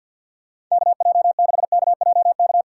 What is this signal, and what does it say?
Morse code, heard as a keyed tone, about 47 wpm, 700 Hz. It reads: KJ6KJK
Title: USA Call Signs - 6 Characters - 4x Character Spacing - 50wpm